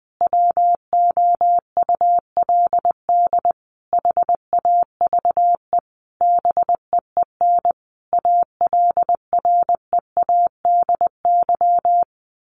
Morse code, heard as a keyed tone, about 20 wpm, 700 Hz. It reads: WOULD HAVE BEEN ALREADY
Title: Sentences from Top 1500 Words 20wpm